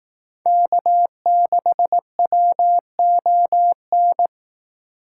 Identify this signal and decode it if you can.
Morse code, heard as a keyed tone, about 18 wpm, 700 Hz. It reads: K6WON
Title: Speed Racing - USA Call Signs 15wpm